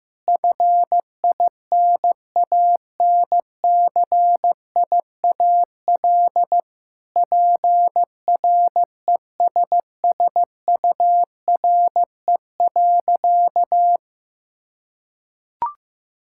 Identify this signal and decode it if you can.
Morse code, heard as a keyed tone, about 15 wpm, 700 Hz. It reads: FINANCIAL PRESSURE.
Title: Sentences from Top 1000 Words - Review Part 6 - 15wpm